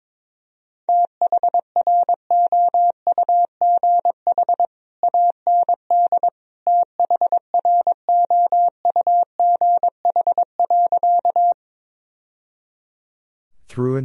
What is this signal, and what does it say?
Morse code, heard as a keyed tone, about 22 wpm, 700 Hz. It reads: THROUGH AND THROUGH.
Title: Sentences from Top 1000 Words - Review Part 1 - 22wpm